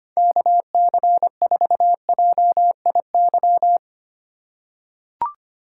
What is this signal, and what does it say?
Morse code, heard as a keyed tone, about 25 wpm, 700 Hz. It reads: KC4JIY
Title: USA Call Signs - 4x Character Spacing - 25wpm